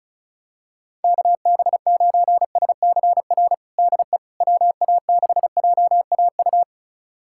Morse code, heard as a keyed tone, about 36 wpm, 700 Hz. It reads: KB9SCR DE WA6JAU
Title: USA Callsign DE Callsign - 6 Characters - 35wpm